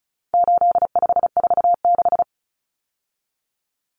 Morse code, heard as a keyed tone, about 34 wpm, 700 Hz. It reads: 8546
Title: Common Cut Numbers - 4-Digit 35wpm